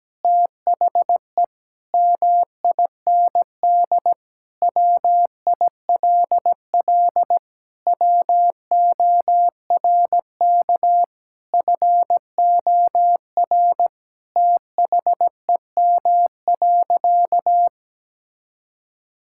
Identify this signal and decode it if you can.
Morse code, heard as a keyed tone, about 17 wpm, 700 Hz. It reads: THE MIND WILL WORK FOR THEM.
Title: Sentences from Top 1000 Words - Review Part 5 - 17wpm